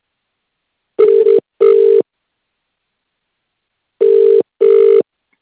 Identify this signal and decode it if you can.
Ringback tone (UK), 400 + 450 Hz.